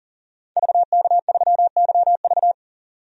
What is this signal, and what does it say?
Morse code, heard as a keyed tone, about 38 wpm, 700 Hz. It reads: VK3YV
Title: DXCC-100 Call Signs - 5 Characters - 40wpm